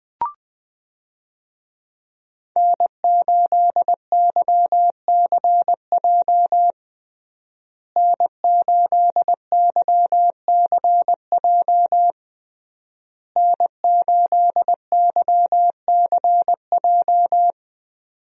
Morse code, heard as a keyed tone, about 21 wpm, 700 Hz. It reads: N8YCJ N8YCJ N8YCJ
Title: USA Call Signs - 5 Characters - Repeated 3x - 20wpm